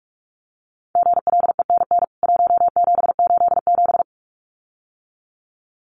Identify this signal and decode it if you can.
Morse code, heard as a keyed tone, about 44 wpm, 700 Hz. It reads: GLENN 1787
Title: CWT Exchange 45wpm